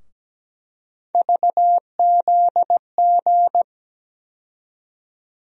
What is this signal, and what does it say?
Morse code, heard as a keyed tone, about 17 wpm, 700 Hz. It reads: VZG
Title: USA Call Sign Postfixes - 3 Letters 17wpm